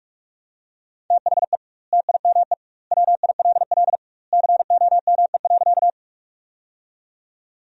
Morse code, heard as a keyed tone, about 44 wpm, 700 Hz. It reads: THE TIME WILL COME.
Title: 2x Word-Spacing - Sentences from Top 100 Words - 45wpm